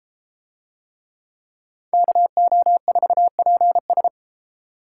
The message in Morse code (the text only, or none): KO4PS